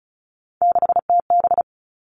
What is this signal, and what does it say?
Morse code, heard as a keyed tone, about 34 wpm, 700 Hz. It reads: 6TB
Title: All Cut Numbers - 3-Digit 35wpm